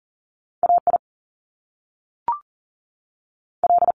Morse code, heard as a keyed tone, about 40 wpm, 700 Hz. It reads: A I L
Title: Top Two Letter Combinations in Words 40wpm